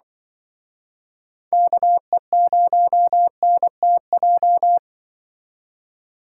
Morse code, heard as a keyed tone, about 25 wpm, 700 Hz. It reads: KE0NTJ
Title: Speed Racing - USA 2x3 Call Signs 20wpm